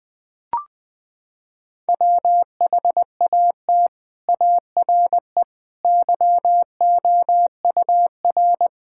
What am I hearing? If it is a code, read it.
Morse code, heard as a keyed tone, about 20 wpm, 700 Hz. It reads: WHAT ARE YOUR
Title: Sets of 6 Words - 20wpm